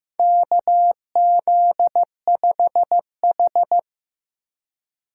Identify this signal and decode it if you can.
Morse code, heard as a keyed tone, about 15 wpm, 700 Hz. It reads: KZ5H
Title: USA Call Signs - 4 Characters - 2x Character Spacing - 15wpm